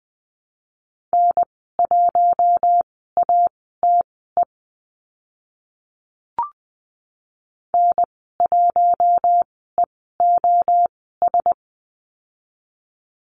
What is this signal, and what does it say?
Morse code, heard as a keyed tone, about 20 wpm, spaced as for 15 wpm, 700 Hz. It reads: N1ATE N1EOS
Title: 042 Lesson 8g - Number 1 - Call Signs 20wpm